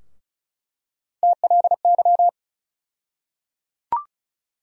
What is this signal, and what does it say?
Morse code, heard as a keyed tone, about 36 wpm, 700 Hz. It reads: TLY
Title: USA Call Sign Postfixes - 3 Letters 35wpm